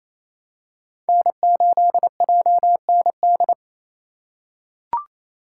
Morse code, heard as a keyed tone, about 28 wpm, 700 Hz. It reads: N8JND
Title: USA Call Signs - 5 Characters - 4x Character Spacing - 28wpm